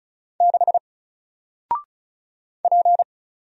Morse code, heard as a keyed tone, about 35 wpm, 700 Hz. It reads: 6 P
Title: Single Letter-Number Alt 35wpm